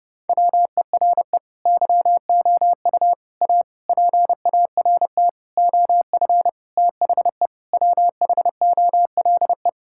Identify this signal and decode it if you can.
Morse code, heard as a keyed tone, about 32 wpm, 700 Hz. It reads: WERE YOU A PART OF THE WHOLE
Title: Sentences from Top 1000 Words - Review Part 6 - 30wpm